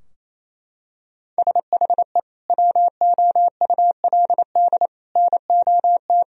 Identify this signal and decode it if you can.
Morse code, heard as a keyed tone, about 28 wpm, 700 Hz. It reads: SHE WOULD NOT
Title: Sentences from Top 1000 Words - Review Part 2 - 28wpm